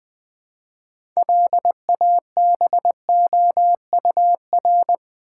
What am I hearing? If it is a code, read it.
Morse code, heard as a keyed tone, about 20 wpm, 700 Hz. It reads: LABOUR
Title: 6 Letter Words - First 3 Letters Given - 20wpm